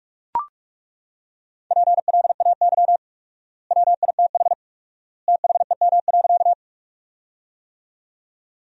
Morse code, heard as a keyed tone, about 43 wpm, 700 Hz. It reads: PLAY WITH THEM.
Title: 4x Word-Spacing - Sentences from Top 200 Words - 45wpm